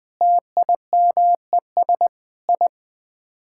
Morse code, heard as a keyed tone, about 20 wpm, 700 Hz. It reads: TIMES I
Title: Sets of 8 Words - 20wpm